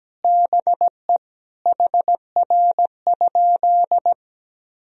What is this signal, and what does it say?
Morse code, heard as a keyed tone, about 17 wpm, 700 Hz. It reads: BE HR?
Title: Sentences from Top 400 Words and Abbreviations 17wpm